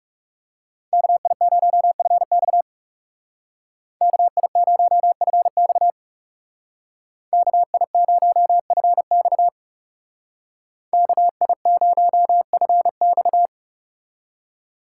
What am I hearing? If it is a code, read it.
Morse code, heard as a keyed tone, about 36 wpm, 700 Hz. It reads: KI0FX KI0FX KI0FX KI0FX
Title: Speed Racing - USA Call Signs 30wpm